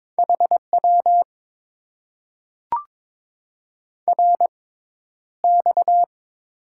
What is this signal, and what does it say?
Morse code, heard as a keyed tone, about 21 wpm, spaced as for 11 wpm, 700 Hz. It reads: HW RX
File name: Top 90 QSO Elements - 6x Character Spacing - 22wpm